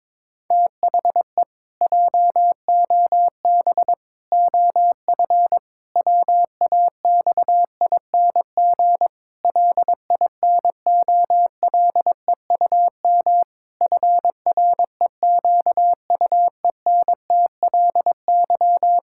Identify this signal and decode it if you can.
Morse code, heard as a keyed tone, about 22 wpm, 700 Hz. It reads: THE JOB OF WAXING LINOLEUM FREQUENTLY
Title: Pangram Challenge v2 - 22wpm